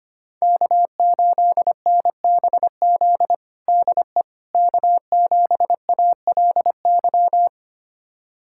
Morse code, heard as a keyed tone, about 25 wpm, 700 Hz. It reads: K8NBZ DE K7ALY
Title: USA Callsign DE Callsign - 5 Characters - 25wpm